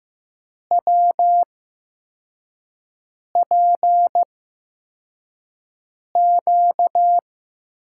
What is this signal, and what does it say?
Morse code, heard as a keyed tone, about 15 wpm, spaced as for 4 wpm, 700 Hz. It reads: WPQ
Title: USA Call Signs - 5 Characters - 8x Character Spacing - 15wpm